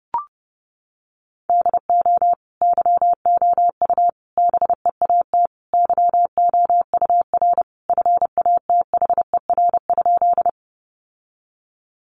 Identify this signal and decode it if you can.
Morse code, heard as a keyed tone, about 30 wpm, 700 Hz. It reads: DO YOU BEAT YOUR FATHER?
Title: Sentences from Top 1300 Words 30wpm